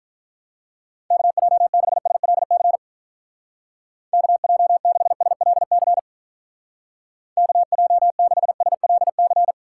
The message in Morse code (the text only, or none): KJ6SLC KJ6SLC KJ6SLC